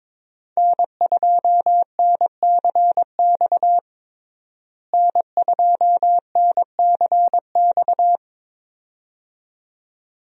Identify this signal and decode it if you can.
Morse code, heard as a keyed tone, about 22 wpm, 700 Hz. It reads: N2NCX N2NCX
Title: USA Call Signs - 5 Characters - Repeated 2x - 22wpm